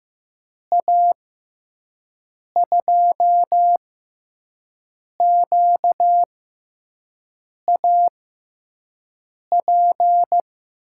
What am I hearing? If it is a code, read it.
Morse code, heard as a keyed tone, about 15 wpm, spaced as for 5 wpm, 700 Hz. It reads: A2QAP
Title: USA Call Signs - 6 Characters - 6x Character Spacing - 15wpm